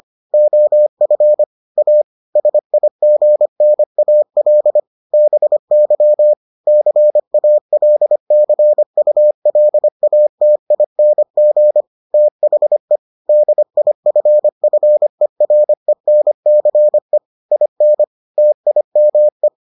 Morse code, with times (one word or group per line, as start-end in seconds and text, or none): 0.3-1.4 OF
1.8-2.0 A
2.4-4.8 SIGNAL
5.1-6.3 BY
6.7-11.8 CALCULATING
12.1-13.0 THE
13.3-17.2 DIFFERENCE
17.5-18.0 IN
18.4-19.5 TIME